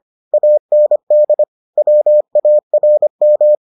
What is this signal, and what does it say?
Morse code, heard as a keyed tone, about 25 wpm, 600 Hz. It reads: AND WARM